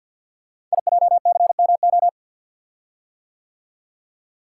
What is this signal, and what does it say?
Morse code, heard as a keyed tone, about 50 wpm, 690 Hz. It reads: IJCMO
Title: Code Groups - 5 Letters - 50wpm